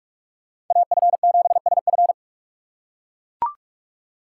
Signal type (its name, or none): Morse code